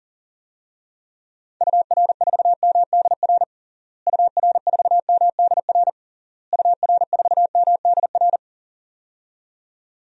Morse code, heard as a keyed tone, about 40 wpm, 700 Hz. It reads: UR4MDR UR4MDR UR4MDR
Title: DXCC-100 Call Signs - 6 Characters - Repeated 3x - 40wpm